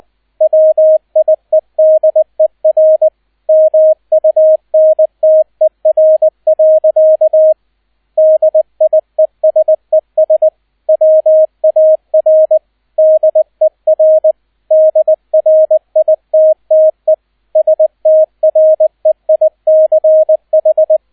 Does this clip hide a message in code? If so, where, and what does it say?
Yes: Morse code from 0.4 to 21.0 s, heard as a keyed tone, about 20 wpm, 630 Hz, with radio noise. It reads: WIEDER MUNTER. DIESES WAR DER DRITTE STREICH